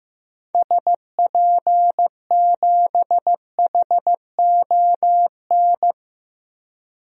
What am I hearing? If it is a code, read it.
Morse code, heard as a keyed tone, about 15 wpm, 700 Hz. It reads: SP7HON
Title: DXCC-100 Call Signs - 6 Characters - Repeated 2x - 15wpm